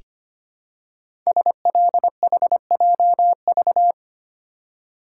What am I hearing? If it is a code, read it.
Morse code, heard as a keyed tone, about 25 wpm, 700 Hz. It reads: SLHJV
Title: Code Groups - 5 Letters - 25wpm